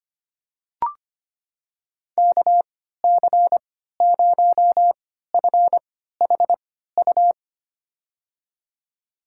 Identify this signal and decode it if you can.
Morse code, heard as a keyed tone, about 25 wpm, spaced as for 14 wpm, 700 Hz. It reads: KC0FHU
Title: USA Call Signs - 6 Characters - 3x Character Spacing - 25wpm